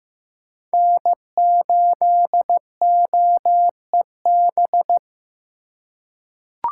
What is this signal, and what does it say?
Morse code, heard as a keyed tone, about 15 wpm, 700 Hz. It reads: N8OEB
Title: USA Call Signs - 4x Character Spacing - 15wpm